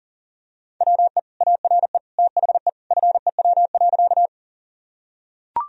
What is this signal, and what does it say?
Morse code, heard as a keyed tone, about 37 wpm, 700 Hz. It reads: WE ARE THE FEW.
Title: Sentences from Top 1000 Words - Review Part 1 - 40wpm